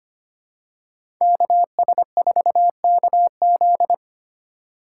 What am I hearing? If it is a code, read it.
Morse code, heard as a keyed tone, about 25 wpm, 700 Hz. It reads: KS4KZ
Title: USA Call Signs - 5 Characters - 4x Character Spacing - 25wpm